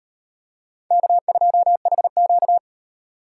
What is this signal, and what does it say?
Morse code, heard as a keyed tone, about 38 wpm, 700 Hz. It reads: K2HQ